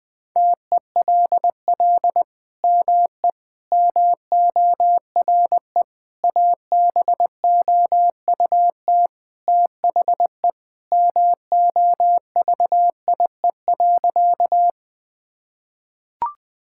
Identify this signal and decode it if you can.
Morse code, heard as a keyed tone, about 20 wpm, 700 Hz. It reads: TELL ME MORE ABOUT THE MOVIE.
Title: Sentences from Top 1000 Words - Review Part 5 - 20wpm